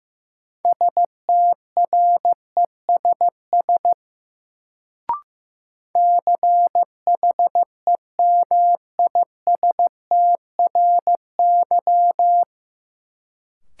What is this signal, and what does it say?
Morse code, heard as a keyed tone, about 15 wpm, 700 Hz. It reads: STRESS CHEMISTRY
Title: Sentences from Top 1000 Words - Review Part 6 - New Word Drill - 15wpm